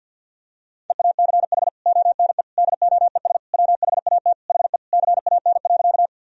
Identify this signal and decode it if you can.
Morse code, heard as a keyed tone, about 52 wpm, 700 Hz. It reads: EACH ONE DOES WHAT HE CAN.
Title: Sentences from Top 1000 Words - Review Part 1 - 50wpm